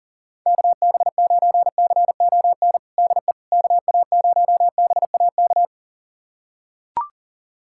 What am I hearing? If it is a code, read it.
Morse code, heard as a keyed tone, about 37 wpm, 700 Hz. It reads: KB9CON DE KA0BAK
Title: USA Callsign DE Callsign 40wpm